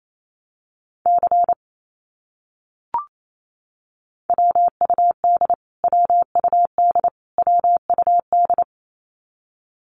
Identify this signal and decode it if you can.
Morse code, heard as a keyed tone, about 28 wpm, 700 Hz. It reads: C WUD WUD WUD
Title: Abbreviations Repeated 3x - 28wpm